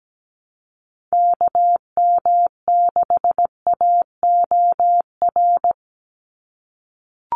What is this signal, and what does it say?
Morse code, heard as a keyed tone, about 17 wpm, 700 Hz. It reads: KM6AOR
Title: USA Call Signs 17wpm